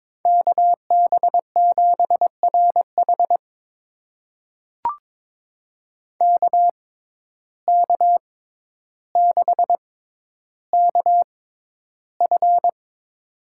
Morse code, heard as a keyed tone, about 22 wpm, spaced as for 11 wpm, 700 Hz. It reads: KB7RH KK6KF
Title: USA Call Signs - 6x Character Spacing - 22wpm